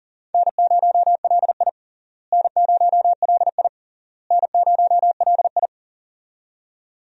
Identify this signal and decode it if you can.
Morse code, heard as a keyed tone, about 40 wpm, 700 Hz. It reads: N0LI N0LI N0LI
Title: USA Call Signs - 4 Characters - Repeated 3x - 40wpm